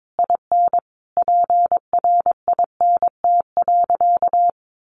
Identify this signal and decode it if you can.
Morse code, heard as a keyed tone, about 22 wpm, 700 Hz. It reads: IN PRINT.
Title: Sentences from Top 1400 Words 22wpm